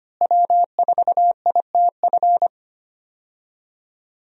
Morse code, heard as a keyed tone, about 25 wpm, 700 Hz. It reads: W4ITF
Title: USA Call Signs - 5 Characters - Repeated 3x - 25wpm